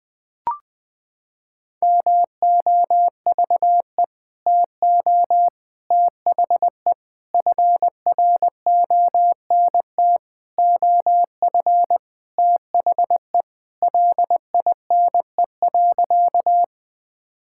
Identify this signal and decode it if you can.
Morse code, heard as a keyed tone, about 20 wpm, 700 Hz. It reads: MOVE TO THE FRONT OF THE LINE.
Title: Sentences with 5-Word Phrases - 20wpm